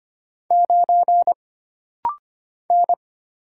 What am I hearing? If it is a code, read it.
Morse code, heard as a keyed tone, about 25 wpm, 700 Hz. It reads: 9 N
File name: Single Character (Letter:Number:Punctuation:Prosign) - Rapid-Fire - 25wpm